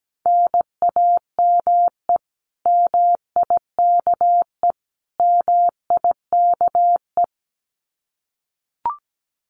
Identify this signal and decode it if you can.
Morse code, heard as a keyed tone, about 17 wpm, 700 Hz. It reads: NAME MIKE MIKE
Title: Common QSO Phrases Encore 17wpm